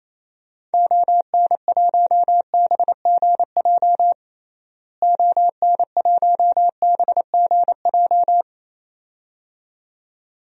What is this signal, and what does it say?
Morse code, heard as a keyed tone, about 28 wpm, 700 Hz. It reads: ON1BGJ ON1BGJ
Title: DXCC-100 Call Signs - 6 Characters - Repeated 2x - 28wpm